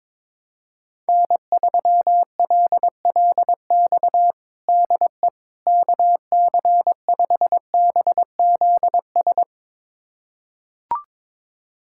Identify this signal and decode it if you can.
Morse code, heard as a keyed tone, about 22 wpm, 700 Hz. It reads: N3LLX DE KC5BZS
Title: USA Callsign DE Callsign 22wpm